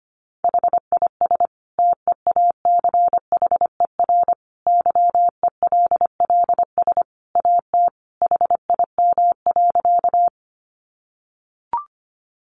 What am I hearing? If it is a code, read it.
Morse code, heard as a keyed tone, about 25 wpm, 700 Hz. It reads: HIS TEACHER YELLS AT HIM.
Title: Additional Easy Phrases 25wpm